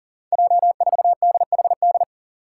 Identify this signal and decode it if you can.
Morse code, heard as a keyed tone, about 39 wpm, 690 Hz. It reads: J4DHD
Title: Five Challenging Letter-Number Combinations 40wpm